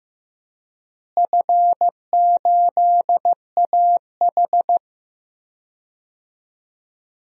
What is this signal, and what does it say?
Morse code, heard as a keyed tone, about 15 wpm, 700 Hz. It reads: F8AH
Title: DXCC-100 Call Signs - 4 Characters - Repeated 3x - 15wpm